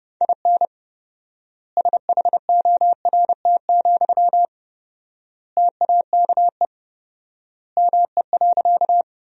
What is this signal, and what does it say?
Morse code, heard as a keyed tone, about 30 wpm, 700 Hz. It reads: IN SHORT, TAKE ME.
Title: 4x Word-Spacing - Sentences from Top 400 Words - 30wpm